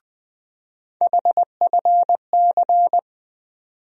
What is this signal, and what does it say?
Morse code, heard as a keyed tone, about 20 wpm, 700 Hz. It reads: HFC
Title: USA Call Sign Postfixes - 3 Letters 20wpm